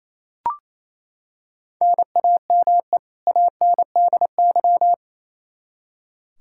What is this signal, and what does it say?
Morse code, heard as a keyed tone, about 28 wpm, 700 Hz. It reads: NAME ANDY
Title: Practice QSO - Exchange 1 - Beginning - 28wpm